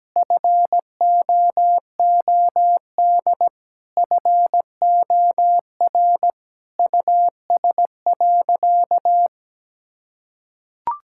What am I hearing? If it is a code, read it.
Morse code, heard as a keyed tone, about 17 wpm, 700 Hz. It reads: FOOD FOR US.
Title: Sentences from Top 1000 Words - Review Part 5 - 17wpm